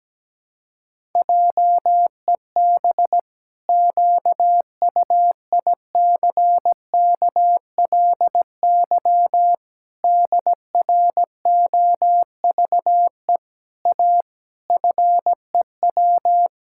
Morse code, heard as a keyed tone, about 17 wpm, 700 Hz. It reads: JEB QUICKLY DROVE A FEW
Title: Pangram Challenge v2 - 17wpm